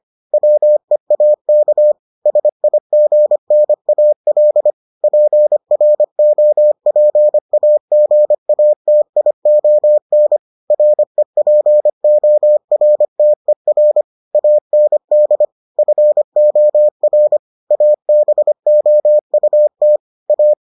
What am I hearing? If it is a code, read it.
Morse code, heard as a keyed tone, about 25 wpm, 600 Hz. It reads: WEAK SIGNAL PROPAGATION REPORTER AND FOR ABOUT A